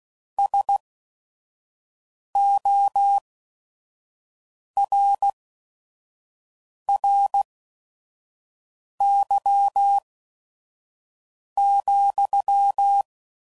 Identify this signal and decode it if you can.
Morse code, heard as a keyed tone, about 16 wpm, spaced as for 5 wpm, 800 Hz. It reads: SORRY,
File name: CW Code Groups @5WPM / QSOs / kc7haa